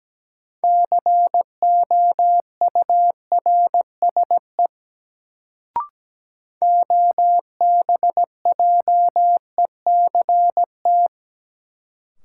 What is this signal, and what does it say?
Morse code, heard as a keyed tone, about 17 wpm, 700 Hz. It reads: COURSE OBJECT
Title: Sentences from Top 500 Words - New Word Drill - 17wpm